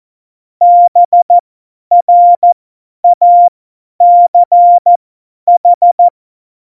Morse code, heard as a keyed tone, about 14 wpm, spaced as for 10 wpm, 700 Hz. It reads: BRACH
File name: Morsekurs / Witze / Witz 18. Februar 2020, 50 F70